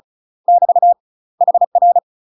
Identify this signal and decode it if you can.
Morse code, heard as a keyed tone, about 36 wpm, 700 Hz. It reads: <BT> HR
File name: UK MASTER QSOs TRAINING / 35WPM / UK MASTER QSOs 0024